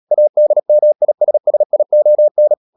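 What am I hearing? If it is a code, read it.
Morse code, heard as a keyed tone, about 37 wpm, 600 Hz. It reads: ADMISSION
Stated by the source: Morse code practice at 35–39 wpm sending English text of Ken Follett's novel Never, from CW med Gnister.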